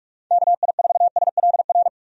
Morse code, heard as a keyed tone, about 47 wpm, 700 Hz. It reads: KI4SLR
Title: USA Call Signs - 6 Characters - 45wpm